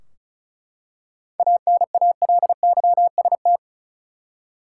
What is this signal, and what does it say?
Morse code, heard as a keyed tone, about 36 wpm, 700 Hz. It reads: ANALYST
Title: K1USN SST QSO Practice 35wpm